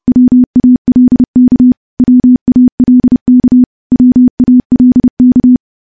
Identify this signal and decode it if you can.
Morse code, heard as a keyed tone, about 30 wpm, 250 Hz. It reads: WALK WALK WALK
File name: morse / morse-training-30WPM